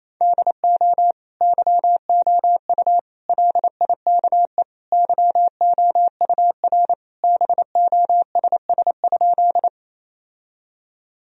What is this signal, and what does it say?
Morse code, heard as a keyed tone, about 28 wpm, 700 Hz. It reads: DO YOU LIKE YOUR BOSS?
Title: Sentences from Top 1000 Words - Review Part 4 - 28wpm